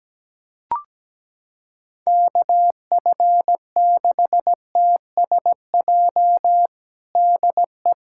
Morse code, heard as a keyed tone, about 17 wpm, 700 Hz. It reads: KF6TSJ DE
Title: USA Callsign DE Callsign - 6 Characters - 17wpm